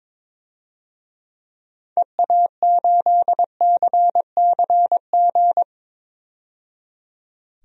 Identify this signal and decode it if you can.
Morse code, heard as a keyed tone, about 22 wpm, 700 Hz. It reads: EA8CCG
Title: DXCC-100 Call Signs - 6 Characters - 22wpm